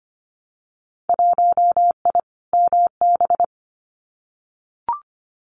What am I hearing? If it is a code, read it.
Morse code, heard as a keyed tone, about 25 wpm, 700 Hz. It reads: 1I MB
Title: Winter Field Day Exchanges Encore 25wpm